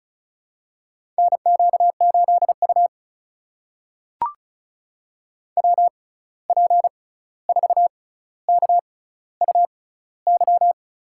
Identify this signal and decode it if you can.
Morse code, heard as a keyed tone, about 36 wpm, spaced as for 16 wpm, 700 Hz. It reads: NQ8U WP4KUY
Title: USA Call Signs - 6x Character Spacing - 35wpm